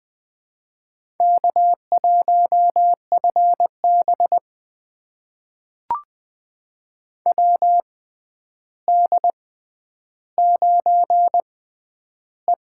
Morse code, heard as a keyed tone, about 20 wpm, spaced as for 10 wpm, 700 Hz. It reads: K1FB WD9E
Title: USA Call Signs - 4 Characters - 6x Character Spacing - 20wpm